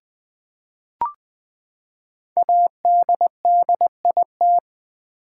Morse code, heard as a keyed tone, about 20 wpm, 700 Hz. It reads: ADDIT
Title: Build A Word - 5 Letters At A Time - 12-Letter Words - 20wpm